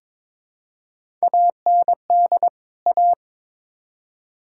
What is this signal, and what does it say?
Morse code, heard as a keyed tone, about 21 wpm, 700 Hz. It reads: AND A
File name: practice / 22wpm / Build A Phrase - 5-Word Phrases - 22wpm